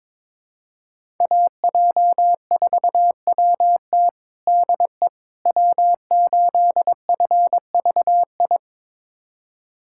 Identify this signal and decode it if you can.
Morse code, heard as a keyed tone, about 22 wpm, 700 Hz. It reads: AJ4WT DE W8FVI
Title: USA Callsign DE Callsign - 5 Characters - 22wpm